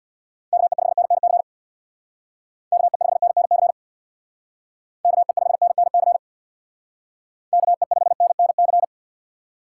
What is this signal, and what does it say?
Morse code, heard as a keyed tone, about 64 wpm, 700 Hz. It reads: KE5NNC KE5NNC KE5NNC KE5NNC